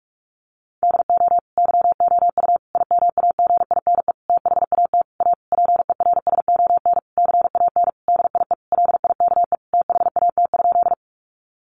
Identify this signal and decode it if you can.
Morse code, heard as a keyed tone, about 44 wpm, 700 Hz. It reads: DO YOU IMAGINE THAT A PERSON CAN DIE LIKE THAT?
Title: Sentences from Top 1400 Words 45wpm